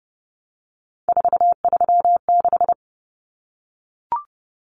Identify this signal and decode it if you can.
Morse code, heard as a keyed tone, about 30 wpm, 700 Hz. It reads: 436
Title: Numbers - 3-Digit 30wpm